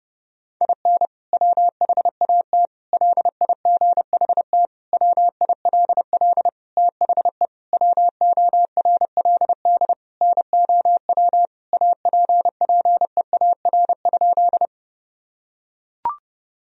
Morse code, heard as a keyed tone, about 30 wpm, 700 Hz. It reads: IN WHAT LIGHT WILL THE WORLD NOW APPEAR?
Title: Sentences from Top 1000 Words - Review Part 2 - 30wpm